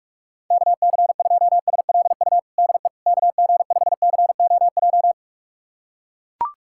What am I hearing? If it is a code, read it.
Morse code, heard as a keyed tone, about 41 wpm, 700 Hz. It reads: KC2SLU DE KG5COJ
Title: USA Callsign DE Callsign - 6 Characters - 45wpm